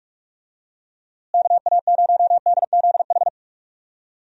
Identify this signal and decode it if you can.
Morse code, heard as a keyed tone, about 46 wpm, 700 Hz. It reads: KA0DZH
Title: USA Call Signs - 6 Characters - Repeated 3x - 45wpm